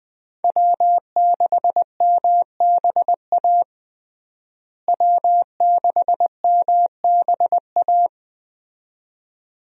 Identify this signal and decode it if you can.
Morse code, heard as a keyed tone, about 20 wpm, 700 Hz. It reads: W6MBA W6MBA
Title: USA Call Signs - 5 Characters - Repeated 2x - 20wpm